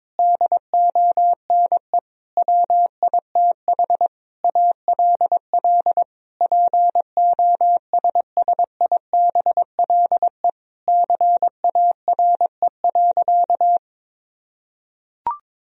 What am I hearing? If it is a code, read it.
Morse code, heard as a keyed tone, about 21 wpm, 700 Hz. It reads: DONE WITH ALL POSSIBLE CARE.
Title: Sentences from Top 1000 Words - Review Part 2 - 22wpm